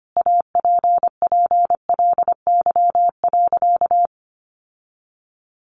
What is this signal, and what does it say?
Morse code, heard as a keyed tone, about 25 wpm, 700 Hz. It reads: APPLY.
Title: Sentences from Top 1400 Words 25wpm